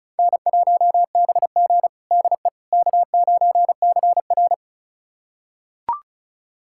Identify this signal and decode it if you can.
Morse code, heard as a keyed tone, about 36 wpm, 700 Hz. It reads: N1BG DE K9CR
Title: USA Callsign DE Callsign - 4 Characters - 35wpm